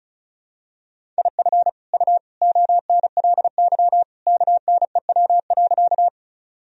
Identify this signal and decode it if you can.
Morse code, heard as a keyed tone, about 34 wpm, 700 Hz. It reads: IF U ONLY KNEW.
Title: Sentences from Top 400 Words and Abbreviations 35wpm